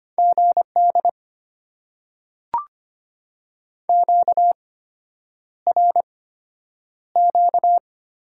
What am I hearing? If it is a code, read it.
Morse code, heard as a keyed tone, about 25 wpm, spaced as for 9 wpm, 700 Hz. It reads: GD QRQ
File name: Top 90 QSO Elements - 8x Character Spacing - 25wpm